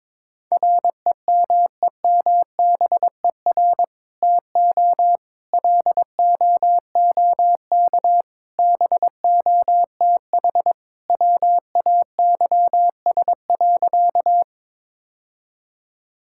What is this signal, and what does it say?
Morse code, heard as a keyed tone, about 22 wpm, 700 Hz. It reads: REMEMBER TO LOOK BOTH WAYS.
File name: practice / 22wpm / Sentences with 2-Word Phrases - 22wpm